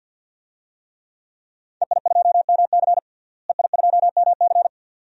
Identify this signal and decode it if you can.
Morse code, heard as a keyed tone, about 52 wpm, 700 Hz. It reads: EI2MC EI2MC
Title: DXCC-100 Call Signs - 5 Characters - Repeated 2x - 50wpm